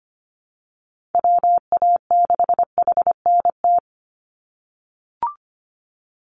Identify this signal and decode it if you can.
Morse code, heard as a keyed tone, about 25 wpm, 700 Hz. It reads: WA6HNT
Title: USA Call Signs 25wpm